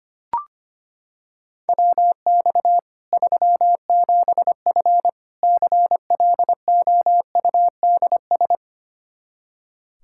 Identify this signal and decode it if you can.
Morse code, heard as a keyed tone, about 25 wpm, 700 Hz. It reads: WX 37F CLOUDS
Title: Practice QSO - Exchange 2 - Info - 25wpm